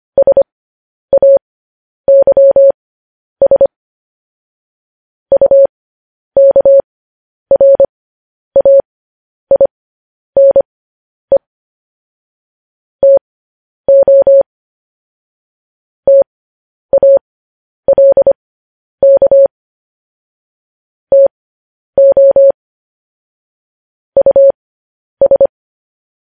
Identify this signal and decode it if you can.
Morse code, heard as a keyed tone, about 25 wpm, spaced as for 10 wpm, 550 Hz. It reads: SAYS UKRAINE TO TALK TO US